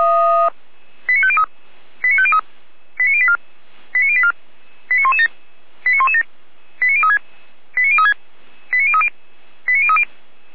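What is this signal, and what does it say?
Five-tone selective call (ZVEI), heard as quick runs of five tones, tones of about 70 ms. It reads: 89503 89503 89084 89084 89117 89117 89036 89036 89030 89030
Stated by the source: ZVEI five-tone call received by SDR.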